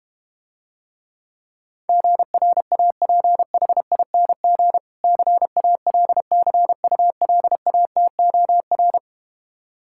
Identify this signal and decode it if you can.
Morse code, heard as a keyed tone, about 32 wpm, 700 Hz. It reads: GRAPHING CALCULATOR